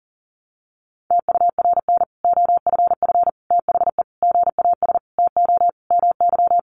Morse code, heard as a keyed tone, about 40 wpm, 700 Hz. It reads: TURN OFF THE GAS TO MY
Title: Sentences from Top 1100 Words 40wpm